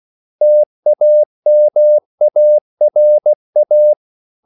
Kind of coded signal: Morse code